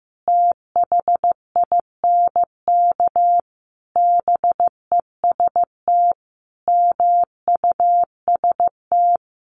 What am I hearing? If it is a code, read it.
Morse code, heard as a keyed tone, about 15 wpm, 700 Hz. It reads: THINK BEST MUST
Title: Sentences from Top 1100 Words 15wpm